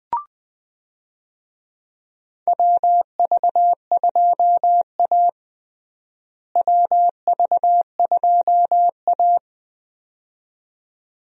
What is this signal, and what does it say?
Morse code, heard as a keyed tone, about 20 wpm, 700 Hz. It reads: WV2A WV2A
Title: USA Call Signs - 4 Characters - Repeated 2x - 20wpm